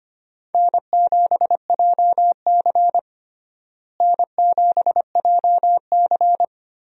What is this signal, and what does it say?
Morse code, heard as a keyed tone, about 25 wpm, 700 Hz. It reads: N7JC N7JC
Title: USA Call Signs - 4 Characters - Repeated 2x - 25wpm